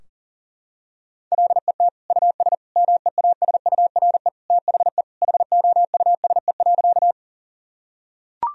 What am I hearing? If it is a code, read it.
Morse code, heard as a keyed tone, about 40 wpm, 700 Hz. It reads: LET US MEASURE THE HOUSE.
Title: Sentences from Top 1000 Words - Review Part 2 - 40wpm